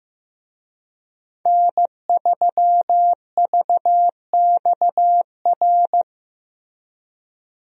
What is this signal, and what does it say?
Morse code, heard as a keyed tone, about 15 wpm, 700 Hz. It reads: N3VXR
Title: USA Call Signs - 5 Characters - Repeated 3x - 15wpm